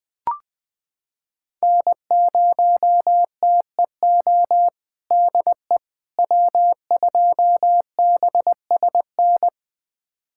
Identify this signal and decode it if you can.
Morse code, heard as a keyed tone, about 21 wpm, 700 Hz. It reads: N0TEO DE W2BSN
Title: USA Callsign DE Callsign - 5 Characters - 20wpm